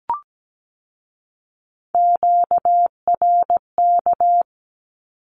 Morse code, heard as a keyed tone, about 17 wpm, 700 Hz. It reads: QRK
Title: Q-Codes with Spoken Meaning - 17wpm